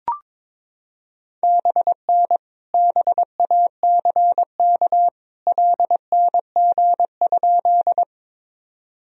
Morse code, heard as a keyed tone, about 22 wpm, 700 Hz. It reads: BN BACK LNG?
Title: Sentences from Top 200 Words and Abbreviations 22wpm